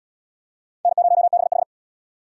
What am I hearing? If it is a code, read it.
Morse code, heard as a keyed tone, about 68 wpm, 700 Hz. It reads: N0BH